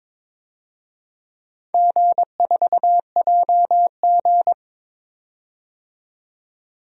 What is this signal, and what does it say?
Morse code, heard as a keyed tone, about 21 wpm, 700 Hz. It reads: G4JG
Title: DXCC-100 Call Signs - 4 Characters - 22wpm